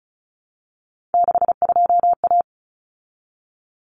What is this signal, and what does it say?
Morse code, heard as a keyed tone, about 34 wpm, 700 Hz. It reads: 62A
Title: Common Cut Numbers - 3-Digit 35wpm